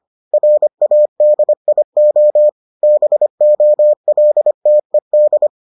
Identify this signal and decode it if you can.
Morse code, heard as a keyed tone, about 25 wpm, 600 Hz. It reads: RADIO BOLTED